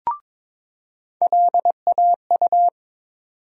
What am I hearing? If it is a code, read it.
Morse code, heard as a keyed tone, about 22 wpm, 700 Hz. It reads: LAU
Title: USA Call Sign Postfixes - 3 Letters 22wpm